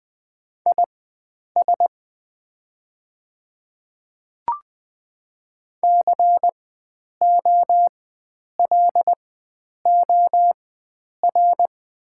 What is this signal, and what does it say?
Morse code, heard as a keyed tone, about 20 wpm, spaced as for 9 wpm, 700 Hz. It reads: IS COLOR
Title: 4x Character Spacing - Top 300 Words - 20wpm